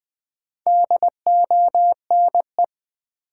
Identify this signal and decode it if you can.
Morse code, heard as a keyed tone, about 20 wpm, 700 Hz. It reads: DONE
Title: Sentences from Top 500 Words - New Word Drill - 20wpm